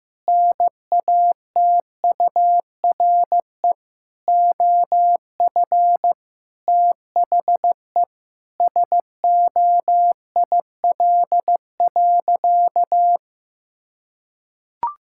Morse code, heard as a keyed tone, about 15 wpm, 700 Hz. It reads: NATURE OF THE SOIL.
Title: Sentences from Top 1600 Words 15wpm